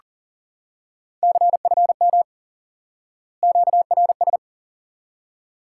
Morse code, heard as a keyed tone, about 40 wpm, 700 Hz. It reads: CFM QRS
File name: IQE Course 3 - Lesson 89 - YAGI - Two Elements - 40wpm